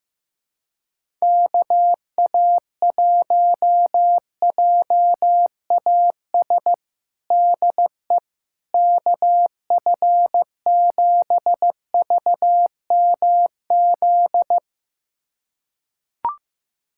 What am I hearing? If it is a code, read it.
Morse code, heard as a keyed tone, about 15 wpm, 700 Hz. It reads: KA1JAS DE KF7VMZ
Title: USA Callsign DE Callsign - 6 Characters - 15wpm